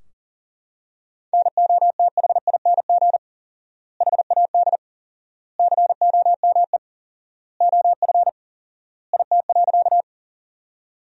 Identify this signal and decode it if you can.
Morse code, heard as a keyed tone, about 40 wpm, 700 Hz. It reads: NOTHING HAD COME OF IT.
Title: 4x Word-Spacing - Sentences from Top 500 Words - 40wpm